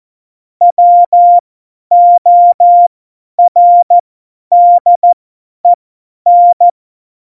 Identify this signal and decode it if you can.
Morse code, heard as a keyed tone, about 14 wpm, spaced as for 10 wpm, 700 Hz. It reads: WORDEN
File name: Morsekurs / Witze / Witz 06 March 2018, 50 F70